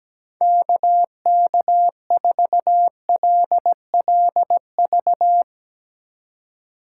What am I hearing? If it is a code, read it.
Morse code, heard as a keyed tone, about 17 wpm, 700 Hz. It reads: KK4LLV
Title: USA Call Signs - 6 Characters - 3x Character Spacing - 17wpm